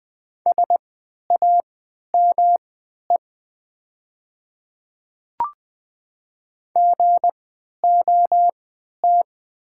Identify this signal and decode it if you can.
Morse code, heard as a keyed tone, about 20 wpm, spaced as for 11 wpm, 700 Hz. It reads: SAME GOT
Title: 3x Character Spacing - Top 300 Words - 20wpm